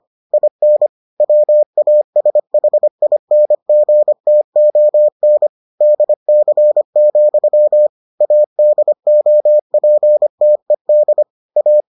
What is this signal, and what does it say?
Morse code, heard as a keyed tone, about 25 wpm, 600 Hz. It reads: IN WASHINGTON DC, ADOPTED A